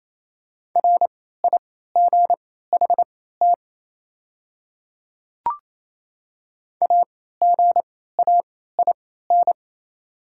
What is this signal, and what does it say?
Morse code, heard as a keyed tone, about 27 wpm, spaced as for 16 wpm, 700 Hz. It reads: RIGHT AGAIN
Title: 3x Character Spacing - Top 200 Words - 28wpm